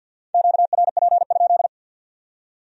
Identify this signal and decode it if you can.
Morse code, heard as a keyed tone, about 61 wpm, 700 Hz. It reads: QRP?
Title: Speed-Racing Q-Codes - 50wpm